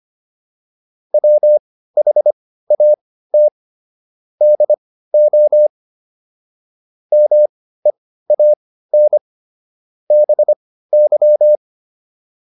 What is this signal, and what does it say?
Morse code, heard as a keyed tone, about 25 wpm, spaced as for 15 wpm, 600 Hz. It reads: WHAT DO MEAN BY